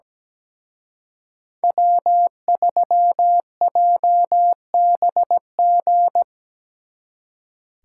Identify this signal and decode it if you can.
Morse code, heard as a keyed tone, about 19 wpm, 700 Hz. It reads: W3JBG
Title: Speed Racing - USA 1x3 Call Signs 17wpm